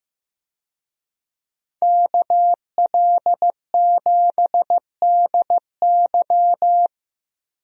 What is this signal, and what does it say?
Morse code, heard as a keyed tone, about 15 wpm, 700 Hz. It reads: KL7DY
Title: DXCC-100 Call Signs - 5 Characters - Repeated 2x - 15wpm